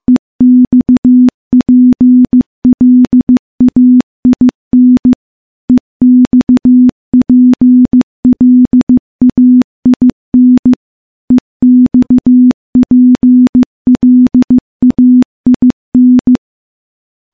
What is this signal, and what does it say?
Morse code, heard as a keyed tone, about 15 wpm, 250 Hz. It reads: EXPLAIN EXPLAIN EXPLAIN
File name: morse / morse-training-15WPM